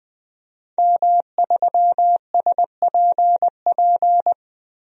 Morse code, heard as a keyed tone, about 21 wpm, 700 Hz. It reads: M3SPP
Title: DXCC-100 Call Signs - 5 Characters - 20wpm